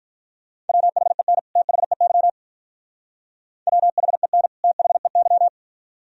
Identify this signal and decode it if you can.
Morse code, heard as a keyed tone, about 44 wpm, 690 Hz. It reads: WHEN THEY WHEN THEY